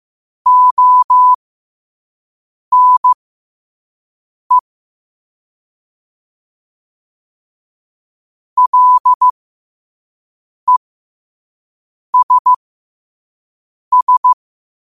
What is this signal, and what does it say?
Morse code, heard as a keyed tone, about 15 wpm, spaced as for 5 wpm, 1000 Hz. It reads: ONE LESS
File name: qotdfiles / 5wpm